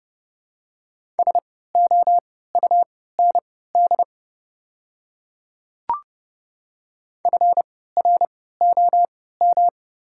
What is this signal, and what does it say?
Morse code, heard as a keyed tone, about 30 wpm, spaced as for 17 wpm, 700 Hz. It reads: SOUND FROM
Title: 3x Character Spacing - Top 200 Words - 30wpm